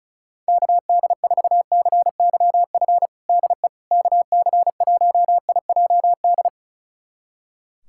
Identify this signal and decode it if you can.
Morse code, heard as a keyed tone, about 36 wpm, 700 Hz. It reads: KD4CYF DE KC1IJD
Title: USA Callsign DE Callsign - 6 Characters - 35wpm